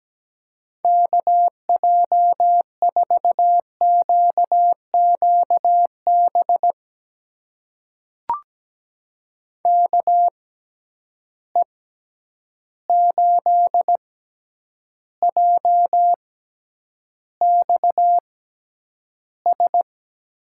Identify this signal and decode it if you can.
Morse code, heard as a keyed tone, about 17 wpm, spaced as for 9 wpm, 700 Hz. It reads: KJ4QQB KE8JXS
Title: USA Call Signs - 6x Character Spacing - 17wpm